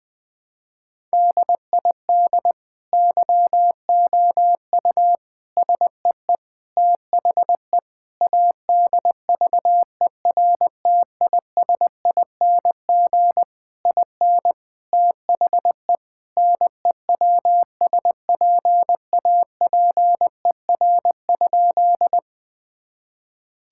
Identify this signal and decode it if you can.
Morse code, heard as a keyed tone, about 21 wpm, 700 Hz. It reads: DID YOU SEE THE ADVERTISING IN THE NEWSPAPER?
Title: Sentences from Top 1000 Words - Review Part 6 - 20wpm